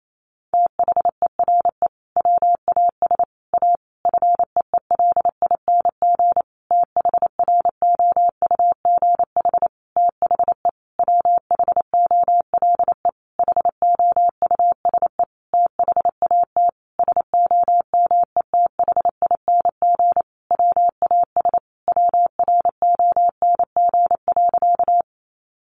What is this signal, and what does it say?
Morse code, heard as a keyed tone, about 28 wpm, 700 Hz. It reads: THERE WAS A FEELING THROUGH THE WHOLE HOUSE THAT SOMETHING WAS WRONG.
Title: Sentences from Top 1200 Words 28wpm